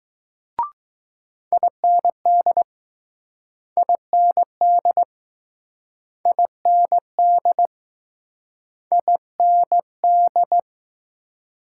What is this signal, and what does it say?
Morse code, heard as a keyed tone, about 20 wpm, 700 Hz. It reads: IND IND IND IND